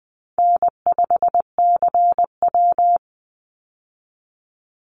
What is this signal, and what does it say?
Morse code, heard as a keyed tone, about 20 wpm, 700 Hz. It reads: N5CW
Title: Easy Call Signs Repeated 2x - 20wpm